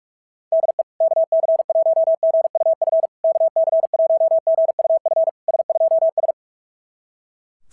Morse code, heard as a keyed tone, about 48 wpm, 660 Hz. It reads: DE KC1GUF KC1GUF S2S
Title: SOTA QSO Practice 45wpm